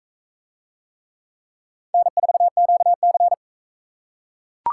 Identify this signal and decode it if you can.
Morse code, heard as a keyed tone, about 42 wpm, 700 Hz. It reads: N4QC